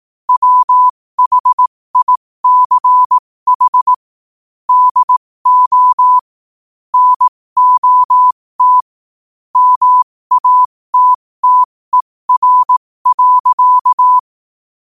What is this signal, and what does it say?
Morse code, heard as a keyed tone, about 18 wpm, spaced as for 15 wpm, 1000 Hz. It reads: WHICH DO NOT MATTER.
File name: qotdfiles / 15wpm